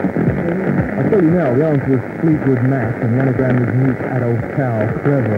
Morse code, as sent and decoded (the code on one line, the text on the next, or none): none